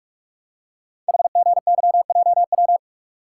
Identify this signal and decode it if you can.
Morse code, heard as a keyed tone, about 44 wpm, 700 Hz. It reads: HGYJW
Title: Code Groups - 5 Letters - 45wpm